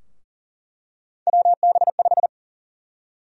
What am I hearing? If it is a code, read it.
Morse code, heard as a keyed tone, about 40 wpm, 700 Hz. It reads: WB5
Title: USA Call Sign Prefixes - Repeated 2x - 40wpm